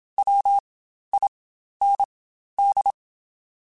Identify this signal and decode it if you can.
Morse code, heard as a keyed tone, about 27 wpm, spaced as for 12 wpm, 800 Hz. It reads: WIND
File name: Spaced Characters / Spaced Characters-mp3s / Weather4 14 sp @25